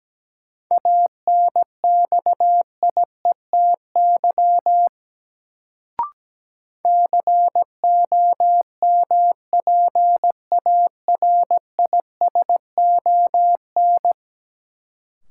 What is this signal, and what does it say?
Morse code, heard as a keyed tone, about 17 wpm, 700 Hz. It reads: ANXIETY COMPARISON
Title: Sentences from Top 1600 Words - New Word Drill Alt - 17wpm